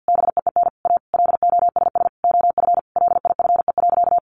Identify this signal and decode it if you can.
Morse code, heard as a keyed tone, about 50 wpm, 700 Hz. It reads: BEEN A LOSS OF LIFE.
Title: Sentences from Top 1400 Words 50wpm